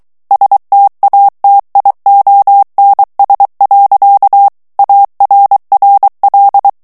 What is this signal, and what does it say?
Morse code, heard as a keyed tone, about 23 wpm, 800 Hz. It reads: STATIONS. ARRL
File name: ANGLAIS texte clair 3 vitesse 22 mots minute